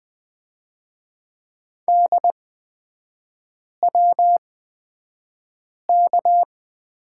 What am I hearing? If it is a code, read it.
Morse code, heard as a keyed tone, about 20 wpm, spaced as for 5 wpm, 700 Hz. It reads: DWK
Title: 243 Lesson 39e - Letter z - Three Characters 20wpm